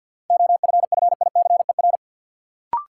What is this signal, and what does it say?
Morse code, heard as a keyed tone, about 49 wpm, 700 Hz. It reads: OFFICER
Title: Sentences from Top 1200 Words - New Word Drill Alt - 50wpm